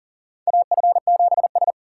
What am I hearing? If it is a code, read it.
Morse code, heard as a keyed tone, about 40 wpm, 700 Hz. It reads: AF7S
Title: USA Call Signs - 4 Characters - 4x Character Spacing - 40wpm